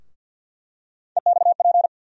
Morse code, heard as a keyed tone, about 50 wpm, 700 Hz. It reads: EXP
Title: USA Call Sign Postfixes - 3 Letters 50wpm